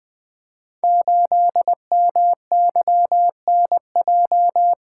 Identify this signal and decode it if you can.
Morse code, heard as a keyed tone, about 20 wpm, 700 Hz. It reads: 8MYNJ
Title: Code Groups - 5 Letter:Numbers - 20wpm